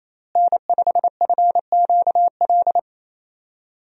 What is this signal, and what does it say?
Morse code, heard as a keyed tone, about 28 wpm, 700 Hz. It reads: N5FQL
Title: USA Call Signs - 5 Characters - Repeated 2x - 28wpm